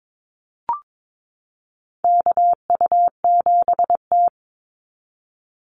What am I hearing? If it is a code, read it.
Morse code, heard as a keyed tone, about 22 wpm, 700 Hz. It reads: KU7T
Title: Call Signs Repeated 3x - 22wpm